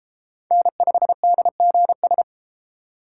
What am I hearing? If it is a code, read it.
Morse code, heard as a keyed tone, about 29 wpm, 700 Hz. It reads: N5DGS